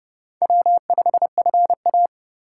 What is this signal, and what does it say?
Morse code, heard as a keyed tone, about 30 wpm, 700 Hz. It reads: W5FA